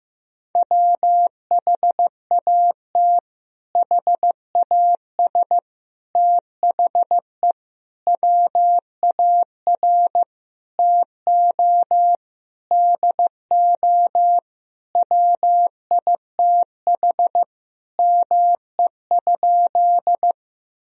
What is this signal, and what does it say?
Morse code, heard as a keyed tone, about 15 wpm, 700 Hz. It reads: WHAT HAS THE WAR TO DO WITH ME?
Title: Sentences from Top 1000 Words - Review Part 2 - 15wpm